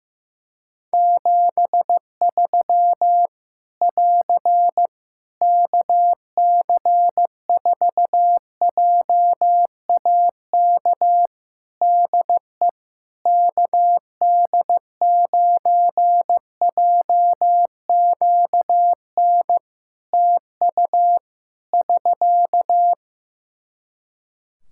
Morse code, heard as a keyed tone, about 15 wpm, 700 Hz. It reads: 73 <AR> KC4JAK DE KD9JQN TU <SK>
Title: Practice QSO - Exchange 4 - Ending - 15wpm